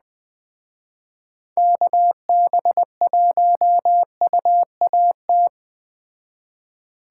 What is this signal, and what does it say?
Morse code, heard as a keyed tone, about 20 wpm, 700 Hz. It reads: KB1UAT
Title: Speed Racing - USA 2x3 Call Signs 17wpm